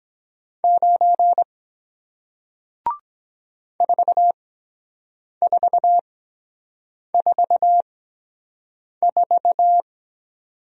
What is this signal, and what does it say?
Morse code, heard as a keyed tone, about 23 wpm, spaced as for 7 wpm, 700 Hz. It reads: 9 4444